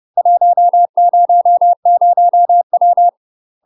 Morse code, heard as a keyed tone, about 30 wpm, 700 Hz. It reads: 100W
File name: UK MASTER QSOs TRAINING / 30WPM / UK MASTER QSOs 0041